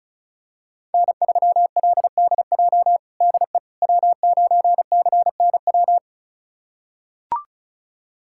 Morse code, heard as a keyed tone, about 36 wpm, 700 Hz. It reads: N3LDJ DE W9CNW
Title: USA Callsign DE Callsign - 5 Characters - 35wpm